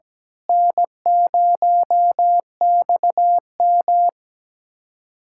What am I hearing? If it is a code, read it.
Morse code, heard as a keyed tone, about 17 wpm, 700 Hz. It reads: N0XM
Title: USA Callsign DE Callsign - 4 Characters - 17wpm